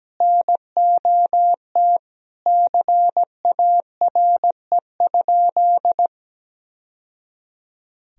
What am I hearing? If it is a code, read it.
Morse code, heard as a keyed tone, about 17 wpm, 700 Hz. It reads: NOT CARE?
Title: Sentences from Top 1000 Words - Review Part 1 - 17wpm